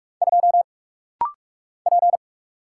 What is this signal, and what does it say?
Morse code, heard as a keyed tone, about 46 wpm, 700 Hz. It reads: 2 P
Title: Single Letter-Number Alt - Rapid-Fire 45wpm